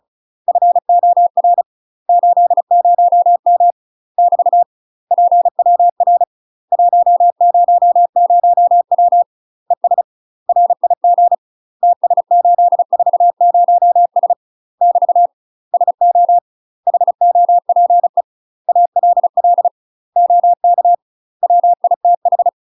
Morse code, heard as a keyed tone, about 34 wpm, 700 Hz. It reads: FOR 80M <BT> PWR 100W ES RIG TS840S <BT> SO HOPE ALL OK WITH
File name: UK MASTER QSOs TRAINING / 35WPM / UK MASTER QSOs 0000